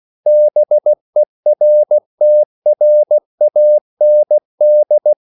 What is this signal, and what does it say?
Morse code, heard as a keyed tone, about 16 wpm, 600 Hz. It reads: BERTRAND
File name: CW / Practice / EnglishQuotations0116